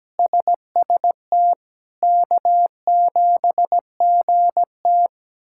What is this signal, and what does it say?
Morse code, heard as a keyed tone, about 17 wpm, 700 Hz. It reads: SST K7GT
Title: K1USN SST QSO Practice - 17wpm